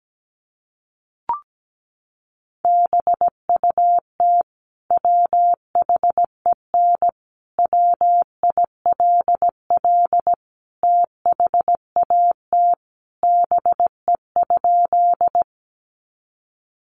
Morse code, heard as a keyed tone, about 17 wpm, 700 Hz. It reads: BUT WHEN WILL THAT BE?
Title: Sentences from Top 100 Words 17wpm